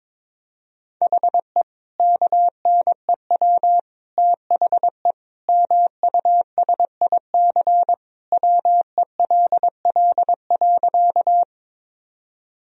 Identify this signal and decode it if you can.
Morse code, heard as a keyed tone, about 22 wpm, 700 Hz. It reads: HE KNEW THE MUSIC WELL.
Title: Sentences from Top 1000 Words - Review Part 1 - 22wpm